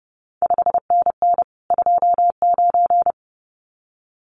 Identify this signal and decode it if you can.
Morse code, heard as a keyed tone, about 30 wpm, 700 Hz. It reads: 5NN 29
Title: CQ WW DX Contest Exchange - 30wpm